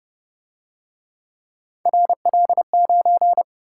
Morse code, heard as a keyed tone, about 31 wpm, 700 Hz. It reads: RL9
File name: Speed-Racing - Three Letter-Number Combinations - 25wpm